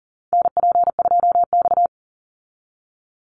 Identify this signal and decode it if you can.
Morse code, heard as a keyed tone, about 40 wpm, 700 Hz. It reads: NP2X
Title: Call Signs Repeated 2x - 40wpm